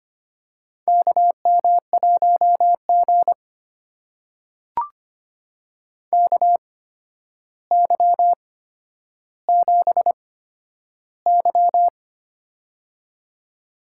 Morse code, heard as a keyed tone, about 25 wpm, spaced as for 11 wpm, 700 Hz. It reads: KM1G KY7Y
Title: USA Call Signs - 4 Characters - 8x Character Spacing - 25wpm